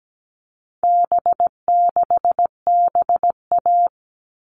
Morse code, heard as a keyed tone, about 17 wpm, 700 Hz. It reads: B6BA
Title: All Cut Numbers - 4-Digit 17wpm